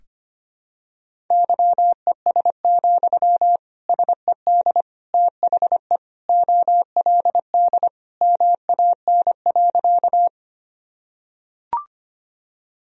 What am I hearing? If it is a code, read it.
Morse code, heard as a keyed tone, about 25 wpm, 700 Hz. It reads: YES, SED THE OLD MAN.
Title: Sentences from Top 500 Words and Abbreviations 25wpm